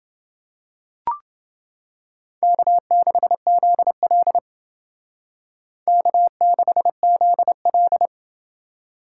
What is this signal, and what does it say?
Morse code, heard as a keyed tone, about 29 wpm, 700 Hz. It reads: K6ZL K6ZL